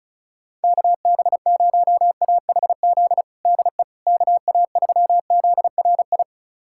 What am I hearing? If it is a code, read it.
Morse code, heard as a keyed tone, about 36 wpm, 700 Hz. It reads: KB0AHZ DE KA3ZRI
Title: USA Callsign DE Callsign - 6 Characters - 35wpm